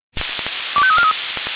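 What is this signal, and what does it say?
Five-tone selective call (ZVEI), heard as quick runs of five tones, tones of about 70 ms. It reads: 25343
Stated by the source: ZVEI five-tone selective call received by SDR.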